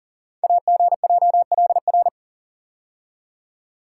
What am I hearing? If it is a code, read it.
Morse code, heard as a keyed tone, about 41 wpm, 690 Hz. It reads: AGJLR
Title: Code Groups - 5 Letters - 40wpm